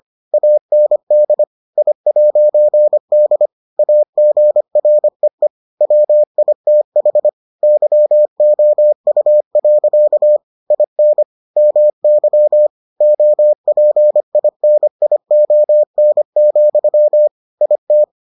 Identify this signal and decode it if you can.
Morse code, heard as a keyed tone, about 25 wpm, 600 Hz. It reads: AND I'D AGREE WITH YOU. IN MY OPINION, IT